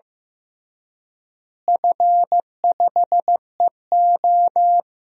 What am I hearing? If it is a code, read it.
Morse code, heard as a keyed tone, about 15 wpm, 700 Hz. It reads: F5EO
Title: DXCC-100 Call Signs - 4 Characters - Repeated 3x - 15wpm